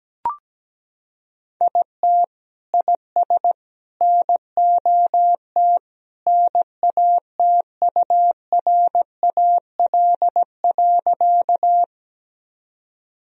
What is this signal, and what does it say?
Morse code, heard as a keyed tone, about 17 wpm, 700 Hz. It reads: IT IS NOT NATURAL.
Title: Sentences from Top 1000 Words - Review Part 4 - 17wpm